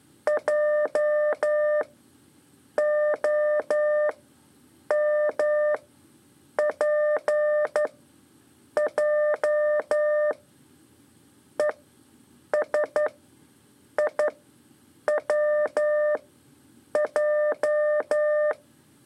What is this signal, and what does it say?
Morse code, heard as a keyed tone, about 11 wpm, spaced as for 7 wpm, 600 Hz. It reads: JOMPJ ESIWJ
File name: 3 lezione CW 2 AIR